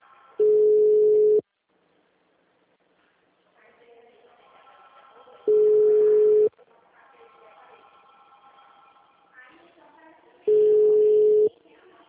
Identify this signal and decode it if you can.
Ringback tone (Europe, 425 Hz), with speech.